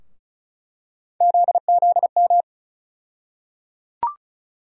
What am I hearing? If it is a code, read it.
Morse code, heard as a keyed tone, about 36 wpm, 700 Hz. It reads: ZZM
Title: USA Call Sign Postfixes - 3 Letters 35wpm